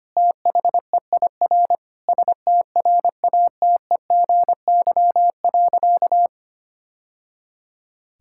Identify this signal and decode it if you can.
Morse code, heard as a keyed tone, about 25 wpm, 700 Hz. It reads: THEIR STRATEGY.
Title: Sentences from Top 1000 Words - Review Part 5 - 25wpm